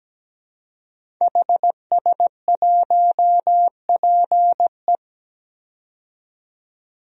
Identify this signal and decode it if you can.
Morse code, heard as a keyed tone, about 17 wpm, 700 Hz. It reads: HS1PE
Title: DXCC-100 Call Signs - 5 Characters - 17wpm